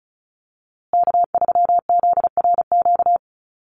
Morse code, heard as a keyed tone, about 34 wpm, 700 Hz. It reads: K3ZRQ
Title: USA Call Signs Repeated 3x 35wpm